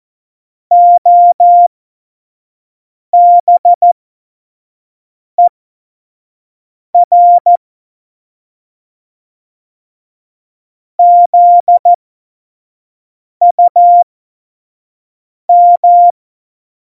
Morse code, heard as a keyed tone, about 14 wpm, spaced as for 5 wpm, 700 Hz. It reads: OBER ZUM
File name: Morsekurs / Witze / Witz 02. Juli 2019, 25 F70